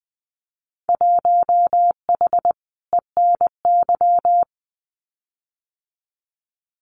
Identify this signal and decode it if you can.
Morse code, heard as a keyed tone, about 20 wpm, 700 Hz. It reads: 1H ENY
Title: Winter Field Day Exchanges 20wpm